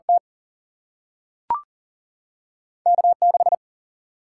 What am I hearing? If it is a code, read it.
Morse code, heard as a keyed tone, about 40 wpm, 700 Hz. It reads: T K6
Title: Single-Letter Single-Number 40wpm